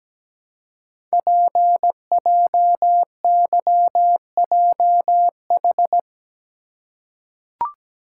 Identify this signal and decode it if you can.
Morse code, heard as a keyed tone, about 17 wpm, 700 Hz. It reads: PJYJH
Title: Code Groups - 5 Letters - 17wpm